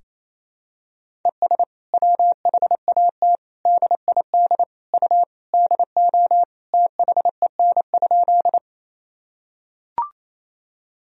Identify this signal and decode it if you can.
Morse code, heard as a keyed tone, about 28 wpm, 700 Hz. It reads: ES WHAT DID U DO THEN?
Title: Sentences from Top 100 Words and Abbreviations 28wpm